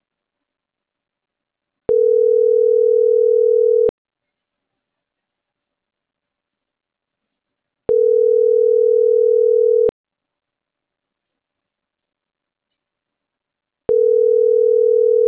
Ringback tone (North America), 440 + 480 Hz.